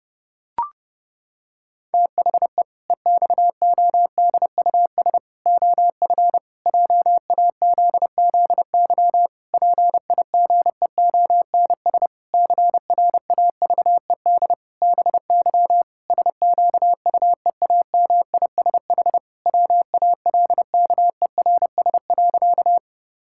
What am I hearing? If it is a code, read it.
Morse code, heard as a keyed tone, about 30 wpm, 700 Hz. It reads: THE EXODUS OF JAZZY PIGEONS CRAVED BY SQUEAMISH WALKERS.
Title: Pangram Challenge v2 - 30wpm